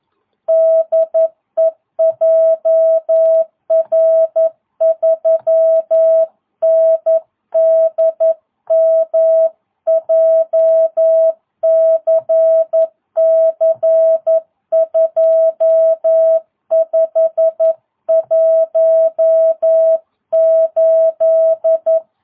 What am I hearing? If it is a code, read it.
Morse code, heard as a keyed tone, about 11 wpm, 650 Hz. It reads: DE JR3NDM JCC 2518
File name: jh8cbh / 2022cw